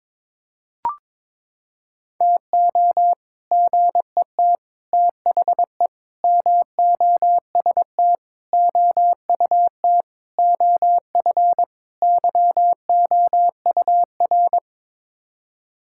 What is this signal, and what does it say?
Morse code, heard as a keyed tone, about 22 wpm, 700 Hz. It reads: TO GET THE MOST OUT OF YOUR
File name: practice / 22wpm / Sets of 7 Words - 22wpm